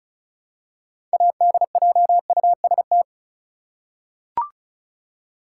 Morse code, heard as a keyed tone, about 36 wpm, 700 Hz. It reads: ADJUST
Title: Six Letter Words - 35wpm